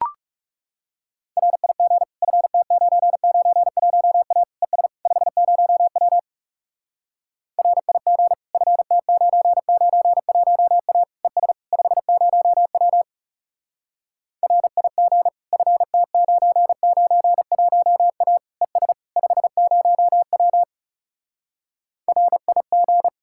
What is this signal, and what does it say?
Morse code, heard as a keyed tone, about 40 wpm, 700 Hz. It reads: RIG FT991A ES 50W RIG FT991A ES 50W RIG FT991A ES 50W RIG